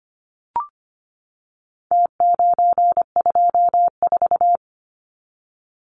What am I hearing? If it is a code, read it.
Morse code, heard as a keyed tone, about 25 wpm, 700 Hz. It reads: T924
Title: Numbers - Contest Style - 4-Digit 25wpm